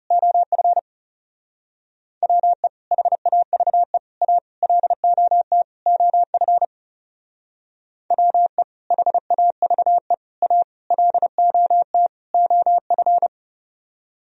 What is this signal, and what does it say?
Morse code, heard as a keyed tone, about 38 wpm, 700 Hz. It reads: OF WE HAVE A LOT OF WE HAVE A LOT OF